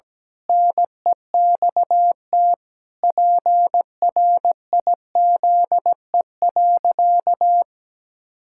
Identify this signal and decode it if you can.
Morse code, heard as a keyed tone, about 17 wpm, 700 Hz. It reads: NEXT PRIZE.
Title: Pangram Challenge v2 - 17wpm